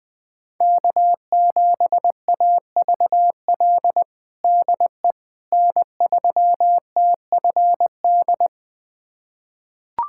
Morse code, heard as a keyed tone, about 20 wpm, 700 Hz. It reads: K7AVL DE N3TFD
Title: USA Callsign DE Callsign - 5 Characters - 20wpm